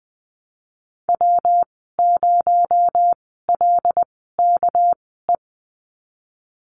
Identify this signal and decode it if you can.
Morse code, heard as a keyed tone, about 20 wpm, spaced as for 15 wpm, 700 Hz. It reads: W0LKE
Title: 231 Lesson 37g - Letter x - Call Signs 20wpm